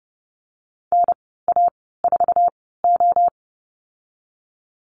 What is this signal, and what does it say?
Morse code, heard as a keyed tone, about 30 wpm, spaced as for 17 wpm, 700 Hz. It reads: NA4O
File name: Lesson 19g - Letter w - Call Signs 30wpm